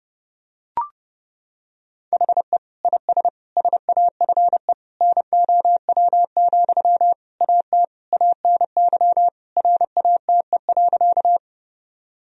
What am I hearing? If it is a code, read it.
Morse code, heard as a keyed tone, about 32 wpm, 700 Hz. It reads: HE IS SAFE NOW, AT ANY RATE.
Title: Sentences from Top 1000 Words - Review Part 4 - 30wpm